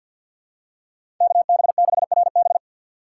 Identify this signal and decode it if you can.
Morse code, heard as a keyed tone, about 47 wpm, 700 Hz. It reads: KB6RB
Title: USA Call Signs - 6x Character Spacing - 50wpm